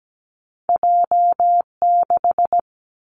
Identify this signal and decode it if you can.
Morse code, heard as a keyed tone, about 17 wpm, 700 Hz. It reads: J6
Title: Two Challenging Letter-Number Combinations 17wpm